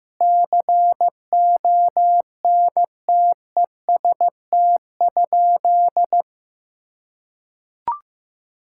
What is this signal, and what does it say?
Morse code, heard as a keyed tone, about 15 wpm, 700 Hz. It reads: CONTEST?
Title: Sentences from Top 1600 Words 15wpm